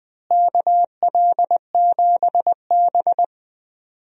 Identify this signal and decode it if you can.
Morse code, heard as a keyed tone, about 21 wpm, 700 Hz. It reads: KL7B
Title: DXCC-100 Call Signs - 4 Characters - Repeated 3x - 20wpm